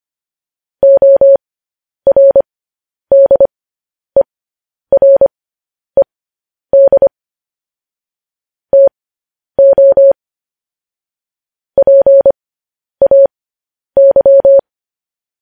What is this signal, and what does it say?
Morse code, heard as a keyed tone, about 25 wpm, spaced as for 10 wpm, 550 Hz. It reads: ORDERED TO PAY
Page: news headlines in morse code at 10 wpm